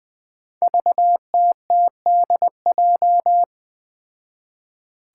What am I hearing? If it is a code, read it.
Morse code, heard as a keyed tone, about 20 wpm, 700 Hz. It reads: VTTDJ
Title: Five Challenging Letter-Number Combinations 20wpm